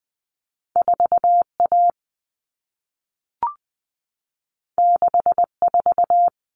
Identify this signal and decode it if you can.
Morse code, heard as a keyed tone, about 20 wpm, 700 Hz. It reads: 4A 64
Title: Common Cut Numbers - 2-Digit 20wpm